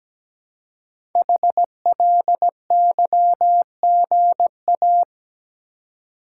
Morse code, heard as a keyed tone, about 17 wpm, 700 Hz. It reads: HLYGA
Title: Code Groups - 5 Letters - 17wpm